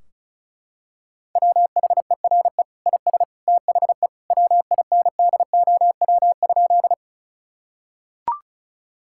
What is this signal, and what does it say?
Morse code, heard as a keyed tone, about 34 wpm, 700 Hz. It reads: WHERE IS THE WINDOW?
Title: Sentences from Top 1600 Words 35wpm